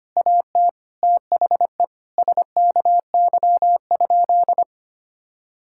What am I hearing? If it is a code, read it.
Morse code, heard as a keyed tone, about 25 wpm, 700 Hz. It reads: AT THE SKY?
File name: practice / 25wpm / Sentences with 4-Word Phrases - 25wpm